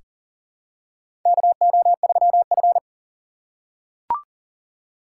Morse code, heard as a keyed tone, about 40 wpm, 700 Hz. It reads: KO3F
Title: USA Call Signs - 4 Characters - 3x Character Spacing - 40wpm